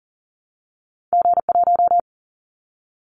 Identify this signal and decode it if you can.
Morse code, heard as a keyed tone, about 40 wpm, 700 Hz. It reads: G1
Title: Callsign Prefixes for Top 25 DXCC - 40wpm